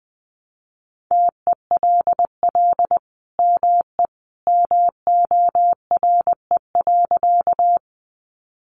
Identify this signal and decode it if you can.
Morse code, heard as a keyed tone, about 20 wpm, 700 Hz. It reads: TELL ME MORE.
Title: Sentences from Top 300 Words 20wpm